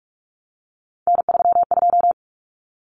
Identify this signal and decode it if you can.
Morse code, heard as a keyed tone, about 44 wpm, 700 Hz. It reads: N32
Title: Common Cut Numbers - 3-Digit 45wpm